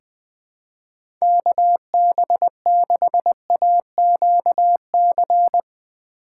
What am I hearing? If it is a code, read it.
Morse code, heard as a keyed tone, about 20 wpm, 700 Hz. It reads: KB6AQC
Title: USA Call Signs - 6 Characters - 8x Character Spacing - 20wpm